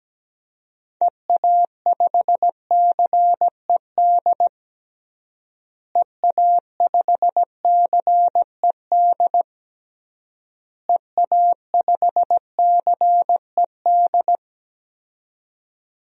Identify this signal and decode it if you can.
Morse code, heard as a keyed tone, about 17 wpm, 700 Hz. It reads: EA5CED EA5CED EA5CED
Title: DXCC-100 Call Signs - 6 Characters - Repeated 3x - 17wpm